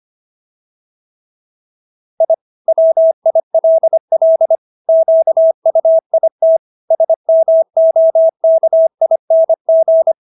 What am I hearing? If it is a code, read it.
Morse code, heard as a keyed tone, about 25 wpm, 650 Hz. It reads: I WILL QUIT SMOKING